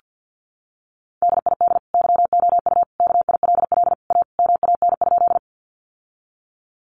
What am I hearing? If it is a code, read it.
Morse code, heard as a keyed tone, about 50 wpm, 700 Hz. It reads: DID YOU KILL A MAN?
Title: Sentences from Top 1400 Words 50wpm